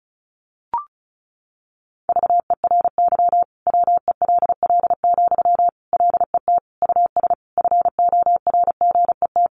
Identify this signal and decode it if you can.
Morse code, heard as a keyed tone, about 34 wpm, 700 Hz. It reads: VERY WELL, LET US FORGET
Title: Sentences from Top 1200 Words 35wpm